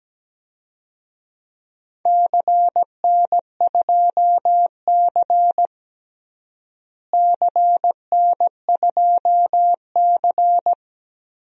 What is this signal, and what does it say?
Morse code, heard as a keyed tone, about 17 wpm, 700 Hz. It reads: CN2C CN2C
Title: DXCC-100 Call Signs - 4 Characters - Repeated 3x - 17wpm